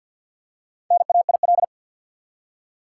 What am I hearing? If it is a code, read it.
Morse code, heard as a keyed tone, about 47 wpm, 700 Hz. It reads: NAIL
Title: Four Letter Words - 50wpm